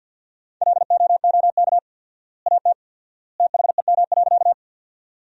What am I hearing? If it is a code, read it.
Morse code, heard as a keyed tone, about 52 wpm, 700 Hz. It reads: LOOK AT THEM.
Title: 4x Word-Spacing - Sentences from Top 100 Words - 50wpm